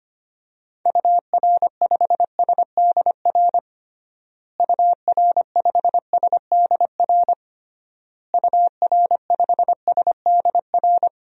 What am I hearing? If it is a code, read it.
Morse code, heard as a keyed tone, about 25 wpm, 700 Hz. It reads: UR5SDR UR5SDR UR5SDR
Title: DXCC-100 Call Signs - 6 Characters - Repeated 3x - 25wpm